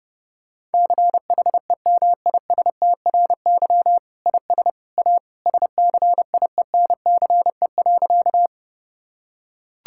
Morse code, heard as a keyed tone, about 32 wpm, 700 Hz. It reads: CHEMISTRY IS A SCIENCE.
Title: Sentences from Top 1000 Words - Review Part 6 - 30wpm